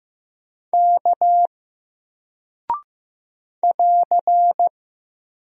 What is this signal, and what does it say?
Morse code, heard as a keyed tone, about 15 wpm, 700 Hz. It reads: K <AR>
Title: Single Character (Letter:Number:Punctuation:Prosign) Alt 15wpm